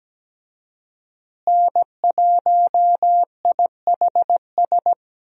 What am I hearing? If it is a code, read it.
Morse code, heard as a keyed tone, about 17 wpm, 700 Hz. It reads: N1IHS
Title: USA Call Signs - 5 Characters - Repeated 3x - 17wpm